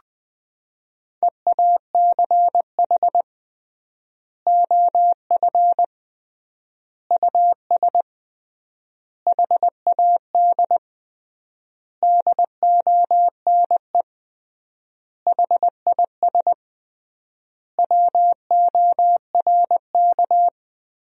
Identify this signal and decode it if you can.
Morse code, heard as a keyed tone, about 21 wpm, 700 Hz. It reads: EACH OF US HAD DONE HIS WORK
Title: 3x Word-Spacing - Sentences from Top 500 Words - 20wpm